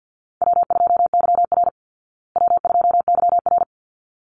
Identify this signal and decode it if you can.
Morse code, heard as a keyed tone, about 48 wpm, 700 Hz. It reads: W2YR W2YR
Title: Easy Call Signs Repeated 2x - 50wpm